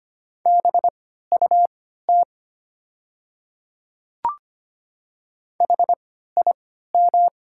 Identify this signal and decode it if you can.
Morse code, heard as a keyed tone, about 25 wpm, spaced as for 14 wpm, 700 Hz. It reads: BUT HIM
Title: 3x Character Spacing - Top 100 Words - 25wpm